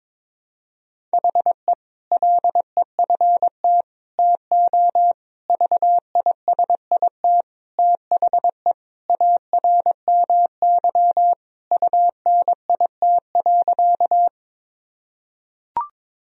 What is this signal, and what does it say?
Morse code, heard as a keyed tone, about 22 wpm, 700 Hz. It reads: HE LEFT TO VISIT THE ARMY UNIT.
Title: Sentences from Top 1000 Words - Review Part 2 - 22wpm